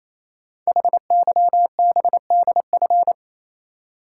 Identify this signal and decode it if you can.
Morse code, heard as a keyed tone, about 29 wpm, 700 Hz. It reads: HYBDF
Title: Code Groups - 5 Letter:Numbers - 28wpm